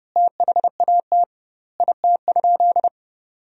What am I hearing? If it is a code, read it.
Morse code, heard as a keyed tone, about 30 wpm, 700 Hz. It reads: THAT IT?
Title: 2x Word-Spacing - Sentences from Top 100 Words - 30wpm